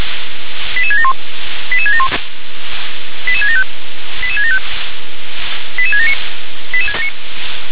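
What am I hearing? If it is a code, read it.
Five-tone selective call (ZVEI), heard as quick runs of five tones, tones of about 70 ms. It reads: 88671 88671 88675 88675 88679
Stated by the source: ZVEI five-tone selective call received by SDR.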